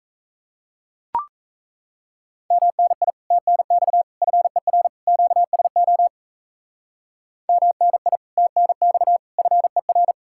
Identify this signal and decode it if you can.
Morse code, heard as a keyed tone, about 34 wpm, 700 Hz. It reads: MNI TNX FER QSO MNI TNX FER
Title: Speed-Racing Common QSO Phrases Encore - 28wpm